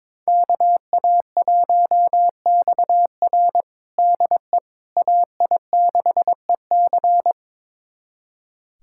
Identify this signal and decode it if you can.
Morse code, heard as a keyed tone, about 22 wpm, 700 Hz. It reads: KA1XR DE AI6EC
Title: USA Callsign DE Callsign - 5 Characters - 22wpm